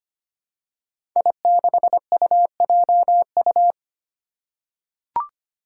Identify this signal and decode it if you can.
Morse code, heard as a keyed tone, about 25 wpm, 700 Hz. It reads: I6UJU
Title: Five Challenging Letter-Number Combinations 25wpm